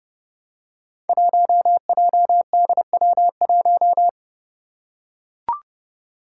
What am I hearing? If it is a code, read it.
Morse code, heard as a keyed tone, about 32 wpm, 700 Hz. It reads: 1JDW1
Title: Five Challenging Letter-Number Combinations 30wpm